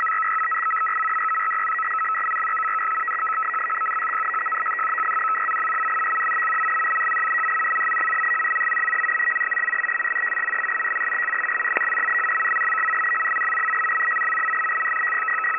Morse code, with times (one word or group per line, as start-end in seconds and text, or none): none